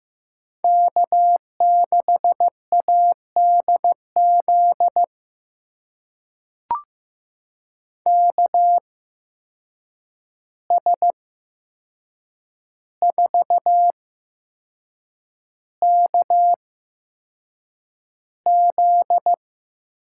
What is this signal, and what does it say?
Morse code, heard as a keyed tone, about 15 wpm, spaced as for 6 wpm, 700 Hz. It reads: K6ADZ KS4KZ
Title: USA Call Signs - 8x Character Spacing - 15wpm